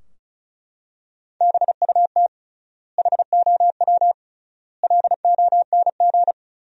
Morse code, heard as a keyed tone, about 36 wpm, 700 Hz. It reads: BUT HOW LONG
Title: 3x Word-Spacing - Sentences from Top 500 Words - 35wpm